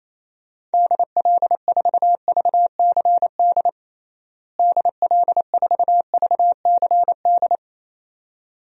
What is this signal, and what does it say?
Morse code, heard as a keyed tone, about 28 wpm, 700 Hz. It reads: DL4VCD DL4VCD
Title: DXCC-100 Call Signs - 6 Characters - Repeated 2x - 28wpm